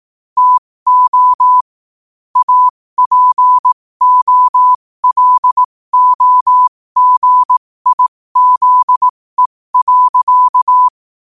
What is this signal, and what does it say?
Morse code, heard as a keyed tone, about 18 wpm, spaced as for 15 wpm, 1000 Hz. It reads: TO APOLOGIZE.
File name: qotdfiles / 15wpm